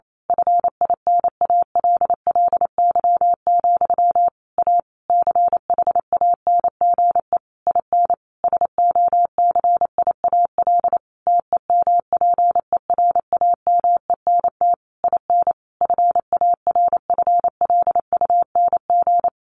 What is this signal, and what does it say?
Morse code, heard as a keyed tone, about 28 wpm, 700 Hz. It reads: FINALLY, A CHANGE IN SOCIAL TEMPERAMENT IN FARFLUNG
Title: A Sound Of Thunder 28wpm